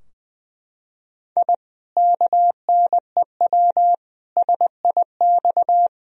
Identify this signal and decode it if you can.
Morse code, heard as a keyed tone, about 20 wpm, 700 Hz. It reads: I KNEW SIX
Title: Sentences from Top 1000 Words - Review Part 2 - 20wpm